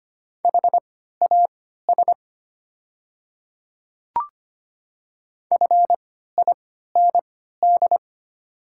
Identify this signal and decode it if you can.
Morse code, heard as a keyed tone, about 25 wpm, spaced as for 14 wpm, 700 Hz. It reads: HAS FIND
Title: 3x Character Spacing - Top 100 Words - 25wpm